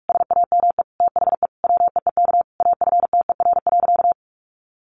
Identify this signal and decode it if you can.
Morse code, heard as a keyed tone, about 44 wpm, 700 Hz. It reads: SAME THE WEEK AFTER.
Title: Sentences from Top 500 Words 45wpm